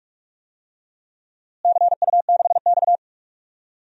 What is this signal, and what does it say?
Morse code, heard as a keyed tone, about 46 wpm, 700 Hz. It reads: CU6X
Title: DXCC-100 Call Signs - 4 Characters - Repeated 3x - 45wpm